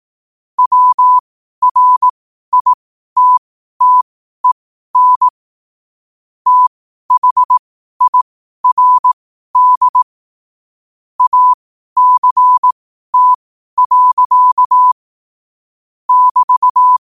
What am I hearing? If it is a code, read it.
Morse code, heard as a keyed tone, about 18 wpm, spaced as for 12 wpm, 1000 Hz. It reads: WRITTEN THIRD ACT. <BT>
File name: qotdfiles / 12wpm